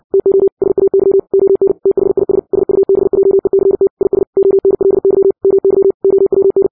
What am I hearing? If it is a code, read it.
Morse code, heard as a keyed tone, about 60 wpm, 390 Hz. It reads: TO SAY ON THIS SUBJECT IS ONLY MY OWN
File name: 60wpm